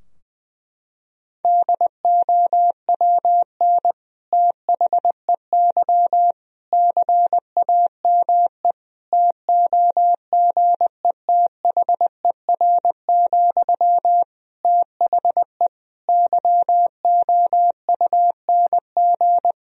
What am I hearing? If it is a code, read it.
Morse code, heard as a keyed tone, about 20 wpm, 700 Hz. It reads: DOWN THEY CAME TOGETHER, THE YOUNG
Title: Sentences from Top 1000 Words - Review Part 2 - 20wpm